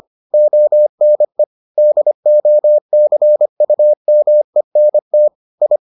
Morse code, heard as a keyed tone, about 25 wpm, 600 Hz. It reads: ONE DOCUMENT I